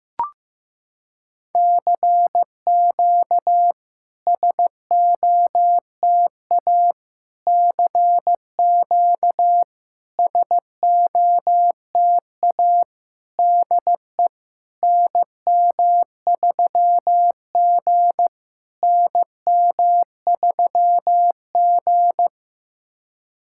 Morse code, heard as a keyed tone, about 15 wpm, 690 Hz. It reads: CQ SOTA CQ SOTA DE NM3G NM3G
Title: SOTA QSO Practice 15wpm